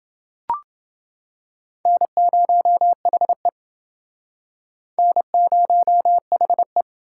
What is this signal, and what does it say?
Morse code, heard as a keyed tone, about 29 wpm, 700 Hz. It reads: N0HE N0HE